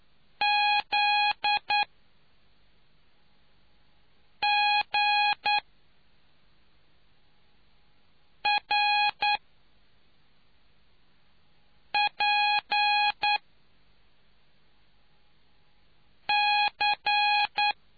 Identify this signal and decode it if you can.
Morse code, heard as a keyed tone, about 9 wpm, spaced as for 3 wpm, 830 Hz. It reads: ZGRPC